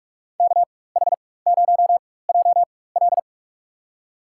Morse code, heard as a keyed tone, about 47 wpm, spaced as for 21 wpm, 700 Hz. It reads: KH0JL
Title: USA Call Signs - 4x Character Spacing - 45wpm